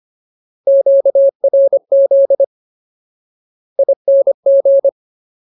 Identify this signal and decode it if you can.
Morse code, heard as a keyed tone, about 25 wpm, 550 Hz. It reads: QRZ ING
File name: Ultimate Word List / 25WPM Ultimate